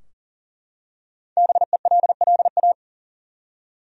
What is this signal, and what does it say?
Morse code, heard as a keyed tone, about 40 wpm, 700 Hz. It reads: BELLA
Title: USA Common Names - 40wpm